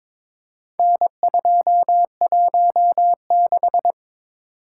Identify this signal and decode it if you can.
Morse code, heard as a keyed tone, about 22 wpm, 700 Hz. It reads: N216
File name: practice / 22wpm / Four Letter-Number Combinations - 22wpm